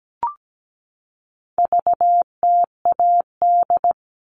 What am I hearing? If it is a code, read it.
Morse code, heard as a keyed tone, about 17 wpm, 700 Hz. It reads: VTAD
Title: All Cut Numbers - 4-Digit 17wpm